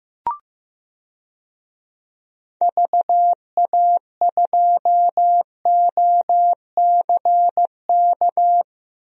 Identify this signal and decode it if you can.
Morse code, heard as a keyed tone, about 15 wpm, 700 Hz. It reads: VA2OCK
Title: DXCC-100 Call Signs - 6 Characters - 15wpm